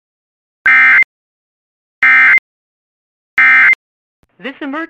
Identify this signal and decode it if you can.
Emergency Alert System (SAME) tones, heard as harsh buzzing bursts, with speech. It is heard from 0.6 to 3.7 s.
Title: Relaying Required Weekly Test